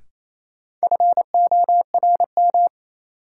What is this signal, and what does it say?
Morse code, heard as a keyed tone, about 28 wpm, 700 Hz. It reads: FORM
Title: Sentences from Top 1000 Words - Review Part 1 - New Word Drill - 28wpm